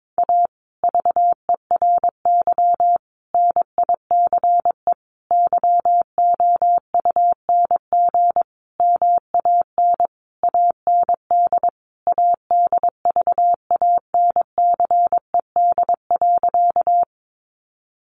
Morse code, heard as a keyed tone, about 22 wpm, 700 Hz. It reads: A VERY NICE YOUNG MAN AND ADVANCED.
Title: Sentences from Top 1200 Words 22wpm